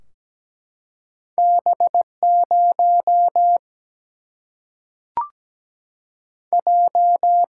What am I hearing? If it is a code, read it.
Morse code, heard as a keyed tone, about 17 wpm, 700 Hz. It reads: B 0 J
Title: Single-Letter Single-Number 17wpm